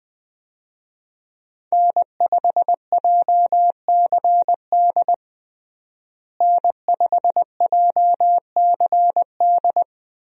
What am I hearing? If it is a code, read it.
Morse code, heard as a keyed tone, about 20 wpm, 700 Hz. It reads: N5JCD N5JCD
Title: USA Call Signs - 5 Characters - Repeated 2x - 20wpm